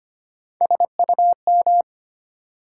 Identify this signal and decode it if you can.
Morse code, heard as a keyed tone, about 25 wpm, 700 Hz. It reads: SUM
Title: Sentences from Top 1000 Words - Review Part 6 - New Word Drill - 25wpm